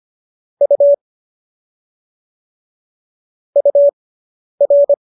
Morse code, heard as a keyed tone, about 25 wpm, spaced as for 10 wpm, 570 Hz. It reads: U UR